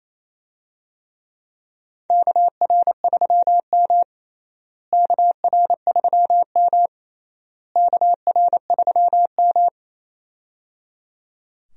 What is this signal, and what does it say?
Morse code, heard as a keyed tone, about 28 wpm, 700 Hz. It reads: KR3M KR3M KR3M
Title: USA Call Signs - 4 Characters - Repeated 3x - 28wpm